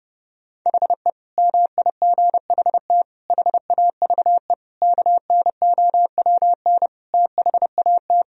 Morse code, heard as a keyed tone, about 32 wpm, 700 Hz. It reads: HE MIGHT HAVE KNOWN THAT
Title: Sentences from Top 1000 Words - Review Part 3 - 30wpm